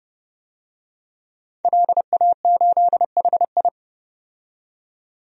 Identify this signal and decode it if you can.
Morse code, heard as a keyed tone, about 30 wpm, 700 Hz. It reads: LA8HI
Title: DXCC-100 Call Signs - 5 Characters - 30wpm